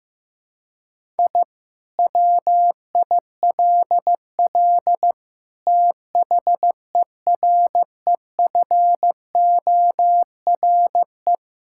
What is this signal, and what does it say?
Morse code, heard as a keyed tone, about 15 wpm, 700 Hz. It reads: I WILL THEREFORE
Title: Sentences from Top 1000 Words - Review Part 5 - 15wpm